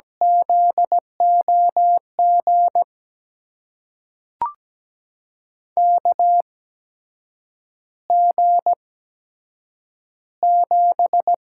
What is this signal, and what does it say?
Morse code, heard as a keyed tone, about 17 wpm, spaced as for 7 wpm, 700 Hz. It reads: ZOG KG7
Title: USA Call Signs - 6 Characters - 8x Character Spacing - 17wpm